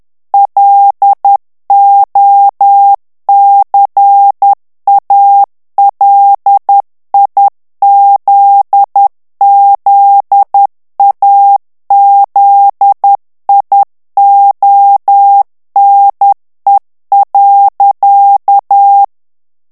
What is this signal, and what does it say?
Morse code, heard as a keyed tone, about 10 wpm, 800 Hz. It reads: LOCALIZZAZIONE.
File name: morse mp3 / ITALIEN texte clair 1 vitesse 10 mots minute